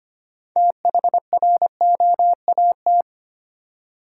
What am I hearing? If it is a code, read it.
Morse code, heard as a keyed tone, about 25 wpm, 700 Hz. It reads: THROAT
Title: Six Letter Words - 25wpm